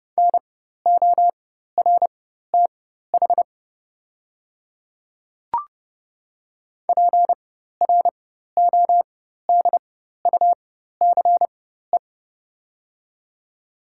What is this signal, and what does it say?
Morse code, heard as a keyed tone, about 30 wpm, spaced as for 14 wpm, 690 Hz. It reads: NORTH PRODUCE
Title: 4x Character Spacing - Top 500 Words - 30wpm